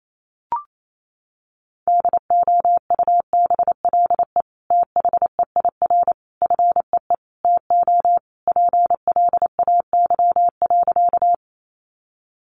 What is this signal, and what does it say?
Morse code, heard as a keyed tone, about 28 wpm, 700 Hz. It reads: DOUBLE THEIR FEE TO PLAY.
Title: Sentences from Top 1500 Words 28wpm